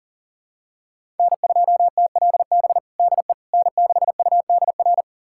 Morse code, heard as a keyed tone, about 40 wpm, 700 Hz. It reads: N2TLB DE N6UDR
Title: USA Callsign DE Callsign - 5 Characters - 40wpm